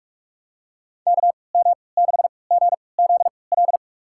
Morse code, heard as a keyed tone, about 43 wpm, spaced as for 25 wpm, 700 Hz. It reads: KM6GZL
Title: USA Call Signs - 6 Characters - 3x Character Spacing - 45wpm